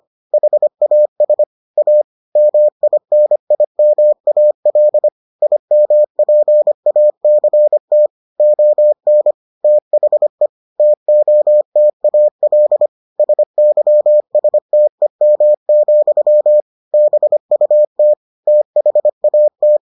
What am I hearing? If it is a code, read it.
Morse code, heard as a keyed tone, about 25 wpm, 600 Hz. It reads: HAS A MINIMAL IMPACT ON THE TOTAL SYSTEM, BUT THAT